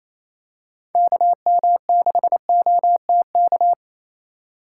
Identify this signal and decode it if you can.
Morse code, heard as a keyed tone, about 28 wpm, 700 Hz. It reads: KM6OTK
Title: USA Call Signs - 6 Characters - 6x Character Spacing - 28wpm